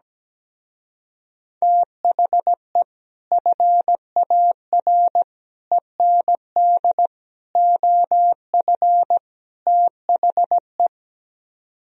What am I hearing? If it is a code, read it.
Morse code, heard as a keyed tone, about 17 wpm, 700 Hz. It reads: THE FAR END OF THE
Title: Speed-Racing - Sets of 5 Words 17wpm